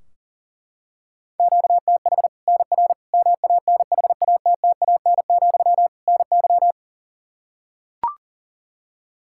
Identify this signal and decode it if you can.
Morse code, heard as a keyed tone, about 37 wpm, 700 Hz. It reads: QTH NR MANHATTAN, NY
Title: Practice QSO - Exchange 1 - Beginning - 40wpm